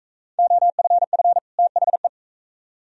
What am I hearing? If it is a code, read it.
Morse code, heard as a keyed tone, about 44 wpm, 700 Hz. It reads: OFF THE